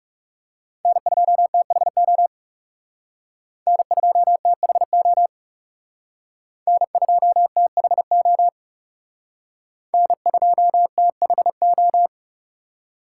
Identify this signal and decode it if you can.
Morse code, heard as a keyed tone, about 39 wpm, 700 Hz. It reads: N2THO N2THO N2THO N2THO